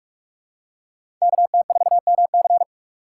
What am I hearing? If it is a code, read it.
Morse code, heard as a keyed tone, about 48 wpm, 700 Hz. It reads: KT4MC
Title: USA Call Signs - 5 Characters - 6x Character Spacing - 45wpm